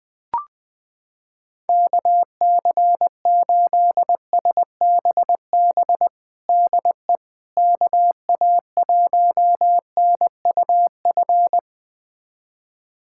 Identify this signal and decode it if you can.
Morse code, heard as a keyed tone, about 21 wpm, 700 Hz. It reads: KC8SBB DE KA1NUF
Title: USA Callsign DE Callsign - 6 Characters - 20wpm